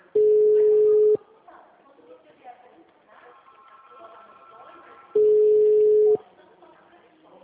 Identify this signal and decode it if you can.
Ringback tone (Europe, 425 Hz), with speech.
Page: Dracena 2 - SP